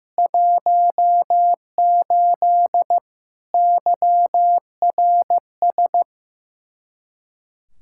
Morse code, heard as a keyed tone, about 15 wpm, 700 Hz. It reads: 18 YRS
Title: Practice QSO - Exchange 3 - More Info - 15wpm